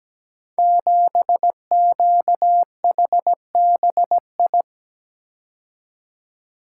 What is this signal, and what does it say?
Morse code, heard as a keyed tone, about 17 wpm, 700 Hz. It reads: 7QHBI
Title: Code Groups - 5 Letter:Numbers - 17wpm